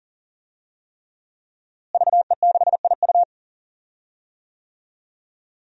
Morse code, heard as a keyed tone, about 41 wpm, 700 Hz. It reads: VE6IU
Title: DXCC-100 Call Signs - 5 Characters - Repeated 3x - 40wpm